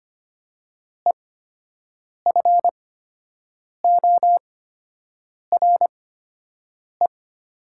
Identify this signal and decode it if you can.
Morse code, heard as a keyed tone, about 25 wpm, spaced as for 7 wpm, 700 Hz. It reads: EFORE
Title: 8x Character Spacing - Top 500 Words - 25wpm